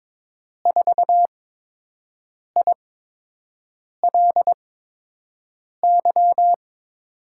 Morse code, heard as a keyed tone, about 22 wpm, spaced as for 6 wpm, 700 Hz. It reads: 4ILY
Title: USA Call Signs - 8x Character Spacing - 22wpm